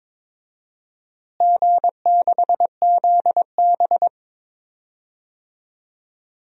Morse code, heard as a keyed tone, about 22 wpm, 700 Hz. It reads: G6ZB
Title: DXCC-100 Call Signs - 4 Characters - Repeated 3x - 22wpm